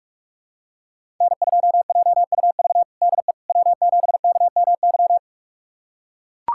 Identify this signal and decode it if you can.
Morse code, heard as a keyed tone, about 43 wpm, 700 Hz. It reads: N2JUV DE W7KMY
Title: USA Callsign DE Callsign - 5 Characters - 45wpm